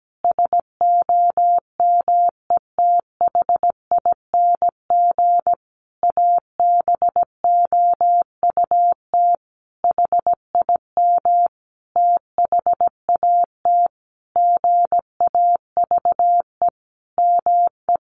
Morse code, heard as a keyed tone, about 17 wpm, 700 Hz. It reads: SOMETHING ABOUT HIM THAT GAVE ME
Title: Sentences from Top 1500 Words 17wpm